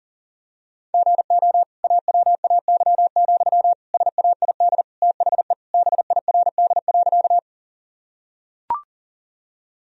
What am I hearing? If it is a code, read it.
Morse code, heard as a keyed tone, about 40 wpm, 700 Hz. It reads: GO AWAY, SAID THE BIRD.
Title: Sentences from Top 1000 Words - Review Part 2 - 40wpm